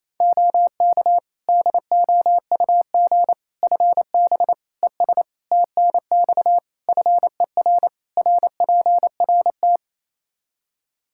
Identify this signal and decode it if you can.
Morse code, heard as a keyed tone, about 28 wpm, 700 Hz. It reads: OK DOUG FB ES TNX FER RPRT
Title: Practice QSO - Exchange 2 - Info - 28wpm